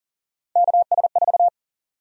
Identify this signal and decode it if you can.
Morse code, heard as a keyed tone, about 40 wpm, 700 Hz. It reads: KS4
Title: USA Call Sign Prefixes - 40wpm